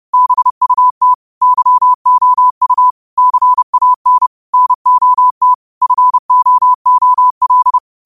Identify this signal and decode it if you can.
Morse code, heard as a keyed tone, about 30 wpm, 1000 Hz. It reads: BUT YOU CAN NOT FOOL
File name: qotdfiles / 30wpm